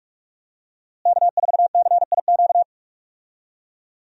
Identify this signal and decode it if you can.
Morse code, heard as a keyed tone, about 46 wpm, 700 Hz. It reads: K4CIQ
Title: USA Call Signs - 5 Characters - Repeated 3x - 45wpm